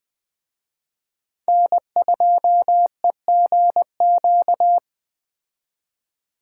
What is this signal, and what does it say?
Morse code, heard as a keyed tone, about 20 wpm, 700 Hz. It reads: N2EGQ
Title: Speed Racing - USA 1x3 Call Signs 17wpm